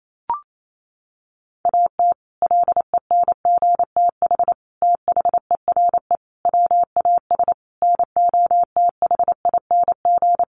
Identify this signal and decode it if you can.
Morse code, heard as a keyed tone, about 28 wpm, 700 Hz. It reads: AT LENGTH THERE WAS NOTHING
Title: Sentences from Top 1000 Words 28wpm